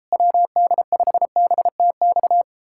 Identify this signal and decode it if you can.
Morse code, heard as a keyed tone, about 38 wpm, 700 Hz. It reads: WD5BTX